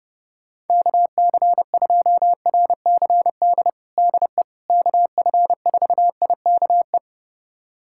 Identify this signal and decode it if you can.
Morse code, heard as a keyed tone, about 32 wpm, 700 Hz. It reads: KC2RCD DE KF4IKE
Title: USA Callsign DE Callsign 30wpm